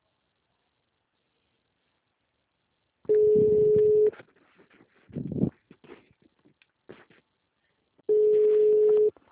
Ringback tone (Europe, 425 Hz).
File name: out- -2002- - - .342